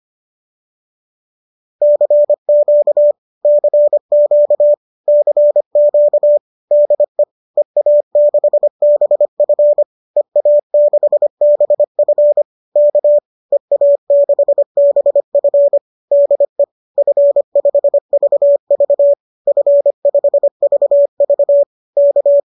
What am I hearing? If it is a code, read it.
Morse code, heard as a keyed tone, about 25 wpm, 590 Hz. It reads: CQ CQ CQ DE EA6BF EA6BF K EA6BF DE F5VV F5VV K